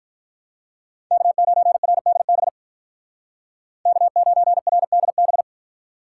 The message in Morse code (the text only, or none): K9RDB K9RDB